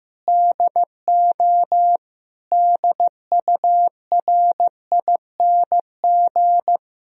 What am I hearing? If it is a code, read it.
Morse code, heard as a keyed tone, about 15 wpm, 700 Hz. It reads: DO DURING
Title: Sentences from Top 1000 Words - Review Part 2 - 15wpm